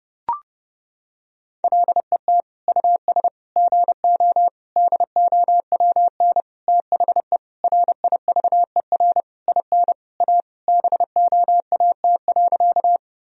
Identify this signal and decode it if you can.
Morse code, heard as a keyed tone, about 30 wpm, 700 Hz. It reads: LET US GO DOWN THE RIVER IN A BOAT.
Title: Sentences from Top 1000 Words - Review Part 3 - 30wpm